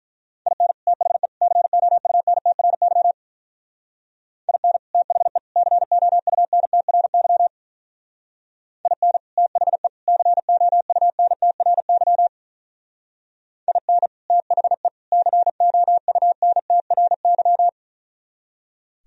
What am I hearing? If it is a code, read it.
Morse code, heard as a keyed tone, about 44 wpm, 700 Hz. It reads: IN THE COUNTRY IN THE COUNTRY IN THE COUNTRY IN THE COUNTRY
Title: Speed-Racing - Sets of 3 Words 35wpm